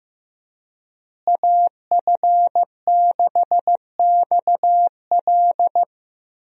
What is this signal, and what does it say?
Morse code, heard as a keyed tone, about 15 wpm, 700 Hz. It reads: AF6XL
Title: USA Call Signs - 5 Characters - Repeated 3x - 15wpm